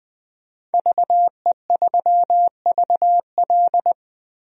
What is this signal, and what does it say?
Morse code, heard as a keyed tone, about 20 wpm, 700 Hz. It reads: VE3VL
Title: DXCC-100 Call Signs - 5 Characters - Repeated 2x - 20wpm